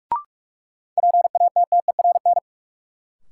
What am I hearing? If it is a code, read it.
Morse code, heard as a keyed tone, about 47 wpm, 700 Hz. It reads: PATTERN
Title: Sentences from Top 1000 Words - New Word Drill - 45wpm